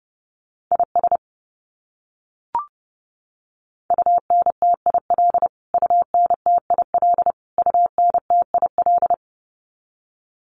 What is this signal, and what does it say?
Morse code, heard as a keyed tone, about 30 wpm, 700 Hz. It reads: IS UNTIL UNTIL UNTIL
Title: Top 500 Words Repeated 30wpm